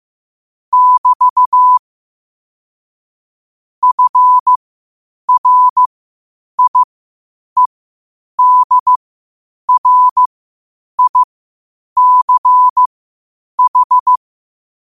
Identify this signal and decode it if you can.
Morse code, heard as a keyed tone, about 15 wpm, spaced as for 8 wpm, 1000 Hz. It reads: <BT> FRIEDRICH
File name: qotdfiles / 8wpm